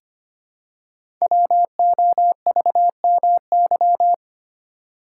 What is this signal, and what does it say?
Morse code, heard as a keyed tone, about 25 wpm, 700 Hz. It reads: WOVMY
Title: Code Groups - 5 Letters - 25wpm